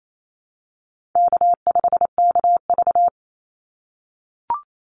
Morse code, heard as a keyed tone, about 28 wpm, 700 Hz. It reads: K5KV
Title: Easy Call Signs - 28wpm